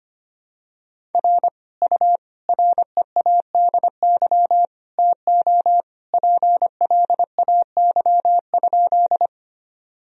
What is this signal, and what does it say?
Morse code, heard as a keyed tone, about 25 wpm, 700 Hz. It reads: R U READY TO PLAY?
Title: Sentences from Top 400 Words and Abbreviations 25wpm